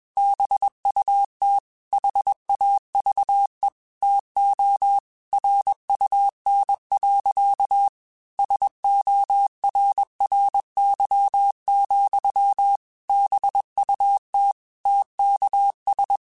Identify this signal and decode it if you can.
Morse code, heard as a keyed tone, about 21 wpm, 800 Hz. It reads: BUT HAVE TO RUN. SORRY, BUT TKS
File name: QSOs / QSOs-mp3s / kc7sac@20